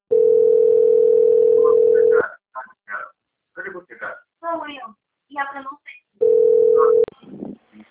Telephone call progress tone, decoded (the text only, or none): ringback tone